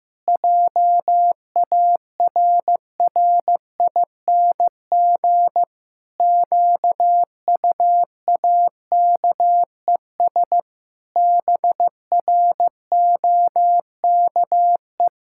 Morse code, heard as a keyed tone, about 15 wpm, 700 Hz. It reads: JARRING QUAKES BROKE
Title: Pangram Challenge v2 - 15wpm